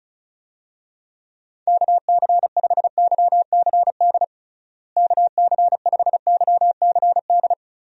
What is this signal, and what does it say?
Morse code, heard as a keyed tone, about 36 wpm, 700 Hz. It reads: KC5YCD KC5YCD
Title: USA Call Signs - 6 Characters - Repeated 2x - 35wpm